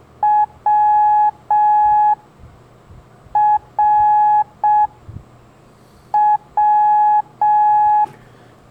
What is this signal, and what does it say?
Morse code, heard as a keyed tone, about 6 wpm, 840 Hz. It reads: W R W
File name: blog / LETRAS COM SOM SEMELHANTE MORSE